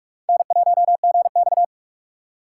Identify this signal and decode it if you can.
Morse code, heard as a keyed tone, about 46 wpm, 700 Hz. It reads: N1GX
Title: USA Call Signs - 4 Characters - 3x Character Spacing - 45wpm